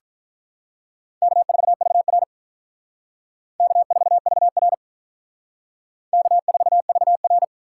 Morse code, heard as a keyed tone, about 43 wpm, 700 Hz. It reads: K4VR K4VR K4VR
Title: Speed Racing - USA Call Signs 35wpm